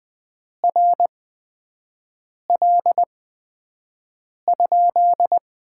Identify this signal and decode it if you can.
Morse code, heard as a keyed tone, about 20 wpm, spaced as for 5 wpm, 700 Hz. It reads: RL?
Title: Top 90 QSO Elements - 8x Character Spacing - 20wpm